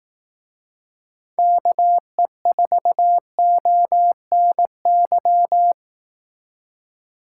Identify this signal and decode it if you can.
Morse code, heard as a keyed tone, about 18 wpm, 700 Hz. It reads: KE4ONY
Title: Speed Racing - USA 2x3 Call Signs 15wpm